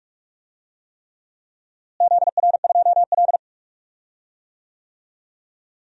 Morse code, heard as a keyed tone, about 44 wpm, 700 Hz. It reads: ZR2L
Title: DXCC-100 Call Signs - 4 Characters - Repeated 3x - 45wpm